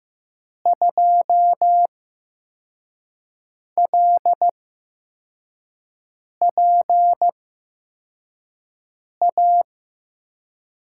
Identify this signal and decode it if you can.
Morse code, heard as a keyed tone, about 15 wpm, spaced as for 4 wpm, 700 Hz. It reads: 2LPA
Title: USA Call Signs - 6 Characters - 8x Character Spacing - 15wpm